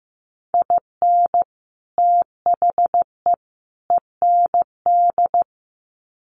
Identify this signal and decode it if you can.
Morse code, heard as a keyed tone, about 15 wpm, 700 Hz. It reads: IN THE END
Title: Sets of 3 Words 15wpm